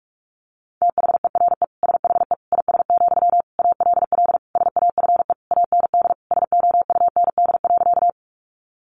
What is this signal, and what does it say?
Morse code, heard as a keyed tone, about 44 wpm, 700 Hz. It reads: THERE SHE IS, ALL SAFE AND SOUND.
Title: Sentences from Top 900 Words 45wpm